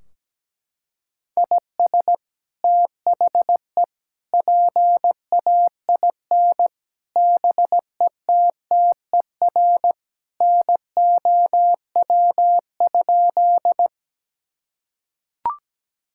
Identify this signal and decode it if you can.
Morse code, heard as a keyed tone, about 17 wpm, 700 Hz. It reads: IS THE PAIN BETTER NOW?
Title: Sentences from Top 1600 Words 17wpm